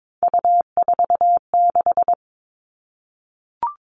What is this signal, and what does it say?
Morse code, heard as a keyed tone, about 22 wpm, 700 Hz. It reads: U46
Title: All Cut Numbers - 3-Digit 22wpm